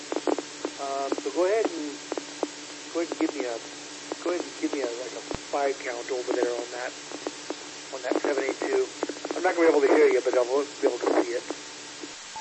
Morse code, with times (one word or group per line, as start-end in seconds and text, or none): none